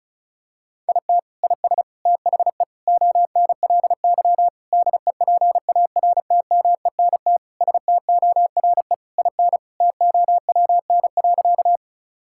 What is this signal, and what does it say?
Morse code, heard as a keyed tone, about 36 wpm, 700 Hz. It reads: IT IS THE ONLY DEPARTMENT STORE IN TOWN.
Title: Sentences from Top 1000 Words - Review Part 5 - 35wpm